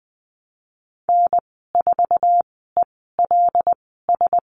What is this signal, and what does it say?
Morse code, heard as a keyed tone, about 20 wpm, spaced as for 15 wpm, 700 Hz. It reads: N4ELS
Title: 231 Lesson 37g - Letter x - Call Signs 20wpm